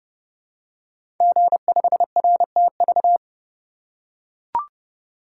Morse code, heard as a keyed tone, about 30 wpm, 700 Hz. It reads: G5RTV
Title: Code Groups - 5 Letter:Numbers - 30wpm